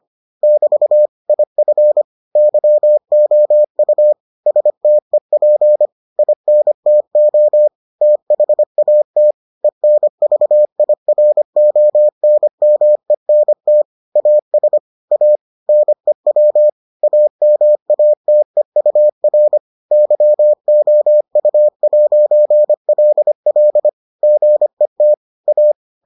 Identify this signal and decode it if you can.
Morse code, heard as a keyed tone, about 25 wpm, 600 Hz. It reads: <BT> IF YOU STEP INTO THAT ENVIRONMENT AS A NEW AMATEUR YOU'LL GET A